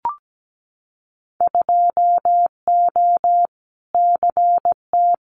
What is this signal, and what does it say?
Morse code, heard as a keyed tone, about 17 wpm, 700 Hz. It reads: 2O CT
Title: Winter Field Day Exchanges 17wpm